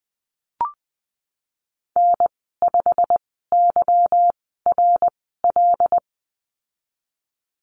Morse code, heard as a keyed tone, about 20 wpm, spaced as for 15 wpm, 700 Hz. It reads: N5YRL
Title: 167 Lesson 27g - Number 7 - Call Signs 20wpm